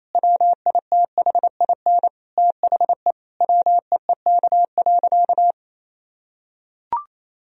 Morse code, heard as a keyed tone, about 28 wpm, 700 Hz. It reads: WITHIN THE WEEK.
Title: Sentences from Top 1000 Words - Review Part 4 - 28wpm